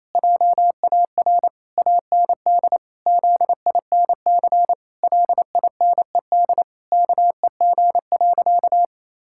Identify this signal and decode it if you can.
Morse code, heard as a keyed tone, about 28 wpm, 700 Hz. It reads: JAR AND ZINC LINED KEG.
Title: Pangram Challenge v2 - 28wpm